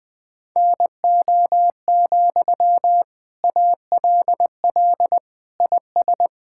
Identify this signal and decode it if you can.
Morse code, heard as a keyed tone, about 19 wpm, 700 Hz. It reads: NO, ALL IS
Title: Sentences from Top 1600 Words 20wpm